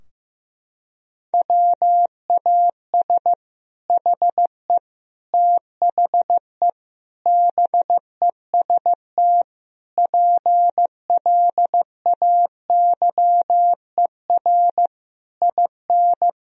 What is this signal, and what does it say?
Morse code, heard as a keyed tone, about 15 wpm, 700 Hz. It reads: WAS HE THE BEST PLAYER IN
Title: Sentences from Top 1000 Words - Review Part 4 - 15wpm